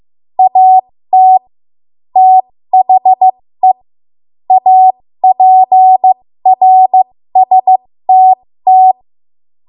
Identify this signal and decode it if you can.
Morse code, heard as a keyed tone, about 15 wpm, spaced as for 13 wpm, 750 Hz. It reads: AT THE APRSTT
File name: Morse / 13WPM